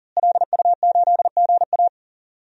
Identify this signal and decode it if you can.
Morse code, heard as a keyed tone, about 40 wpm, 700 Hz. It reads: LU8GA
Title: DXCC-100 Call Signs - 5 Characters - Repeated 2x - 40wpm